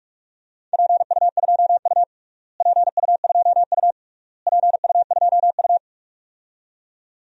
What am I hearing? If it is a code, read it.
Morse code, heard as a keyed tone, about 46 wpm, 700 Hz. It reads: PU2U PU2U PU2U
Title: DXCC-100 Call Signs - 4 Characters - Repeated 3x - 45wpm